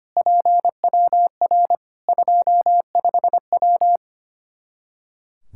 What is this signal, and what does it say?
Morse code, heard as a keyed tone, about 25 wpm, 700 Hz. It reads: PWR 25W
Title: Practice QSO - Exchange 2 - Info - 25wpm